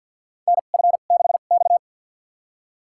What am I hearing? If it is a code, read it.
Morse code, heard as a keyed tone, about 51 wpm, spaced as for 36 wpm, 700 Hz. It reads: NF6X
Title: USA Call Signs - 4 Characters - 2x Character Spacing - 50wpm